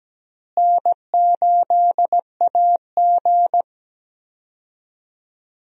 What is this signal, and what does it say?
Morse code, heard as a keyed tone, about 17 wpm, 700 Hz. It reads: N8AG
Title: USA Call Signs - 4 Characters - Repeated 2x - 17wpm